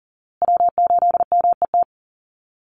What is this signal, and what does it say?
Morse code, heard as a keyed tone, about 40 wpm, 700 Hz. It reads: W8MET
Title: Call Signs Repeated 2x - 40wpm